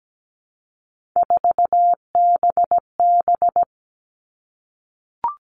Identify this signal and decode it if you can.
Morse code, heard as a keyed tone, about 17 wpm, 700 Hz. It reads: 4BB
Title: All Cut Numbers - 3-Digit 17wpm